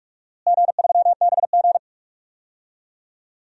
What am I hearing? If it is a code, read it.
Morse code, heard as a keyed tone, about 45 wpm, 700 Hz. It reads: G3BG
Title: DXCC-100 Call Signs - 4 Characters - Repeated 2x - 45wpm